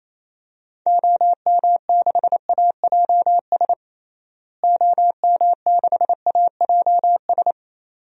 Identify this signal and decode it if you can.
Morse code, heard as a keyed tone, about 28 wpm, 700 Hz. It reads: OM6AJS OM6AJS
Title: DXCC-100 Call Signs - 6 Characters - Repeated 2x - 28wpm